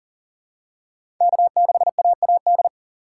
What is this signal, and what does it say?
Morse code, heard as a keyed tone, about 40 wpm, 700 Hz. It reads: K6AAD
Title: USA Call Signs - 5 Characters - 6x Character Spacing - 40wpm